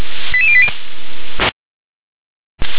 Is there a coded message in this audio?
Yes: five-tone selective call (ZVEI) from 0.3 to 0.7 s, heard as quick runs of five tones, tones of about 70 ms. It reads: 88080